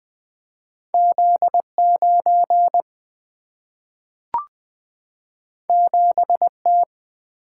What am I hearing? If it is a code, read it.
Morse code, heard as a keyed tone, about 20 wpm, 700 Hz. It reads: Z9 7T
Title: Single-Letter Single-Number 20wpm